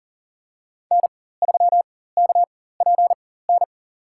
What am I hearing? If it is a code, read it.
Morse code, heard as a keyed tone, about 40 wpm, spaced as for 19 wpm, 700 Hz. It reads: N3KPN
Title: USA Call Signs - 5 Characters - 4x Character Spacing - 40wpm